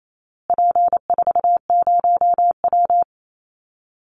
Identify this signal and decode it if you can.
Morse code, heard as a keyed tone, about 28 wpm, 700 Hz. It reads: P40W
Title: Call Signs Repeated 3x - 28wpm